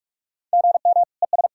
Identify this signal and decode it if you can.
Morse code, heard as a keyed tone, about 44 wpm, 700 Hz. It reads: GM ES
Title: K1USN SST QSO Practice 45wpm